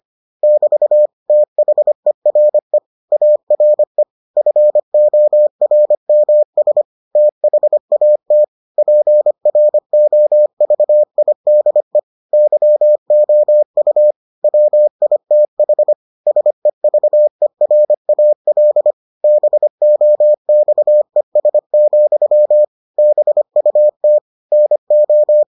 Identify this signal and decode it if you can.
Morse code, heard as a keyed tone, about 25 wpm, 600 Hz. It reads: <BT> THERE ARE FORMS THAT PROVIDE YOU WITH SEVERAL BOXES, BUT NO